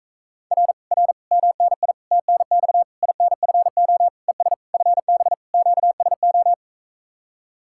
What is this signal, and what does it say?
Morse code, heard as a keyed tone, about 45 wpm, 700 Hz. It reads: R R MNI TNX INFO ES FB QSO